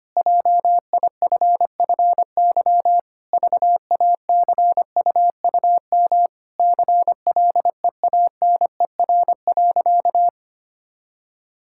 Morse code, heard as a keyed tone, about 25 wpm, 700 Hz. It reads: JIFFY VACUUM CLEANER.
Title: Pangram Challenge v2 - 25wpm